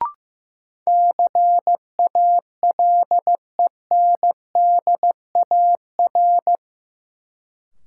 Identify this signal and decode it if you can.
Morse code, heard as a keyed tone, about 15 wpm, 700 Hz. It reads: CALENDAR
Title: Sentences from Top 1600 Words - New Word Drill - 15wpm